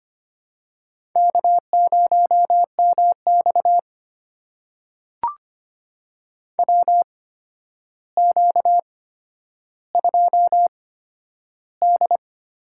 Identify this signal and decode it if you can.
Morse code, heard as a keyed tone, about 25 wpm, spaced as for 11 wpm, 700 Hz. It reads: K0MX WQ2D
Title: USA Call Signs - 4 Characters - 8x Character Spacing - 25wpm